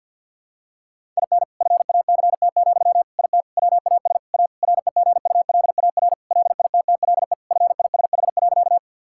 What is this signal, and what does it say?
Morse code, heard as a keyed tone, about 52 wpm, 700 Hz. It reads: IN FACT, IT WAS A REGULAR LITTLE FISH.
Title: Sentences from Top 1000 Words - Review Part 6 - 50wpm